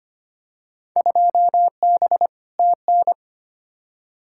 Morse code, heard as a keyed tone, about 25 wpm, 700 Hz. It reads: 2B TN
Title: Field Day Exchanges Encore - 25wpm